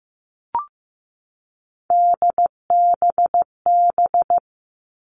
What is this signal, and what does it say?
Morse code, heard as a keyed tone, about 15 wpm, 700 Hz. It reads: DBB
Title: Three Challenging Letter-Number Combinations 15wpm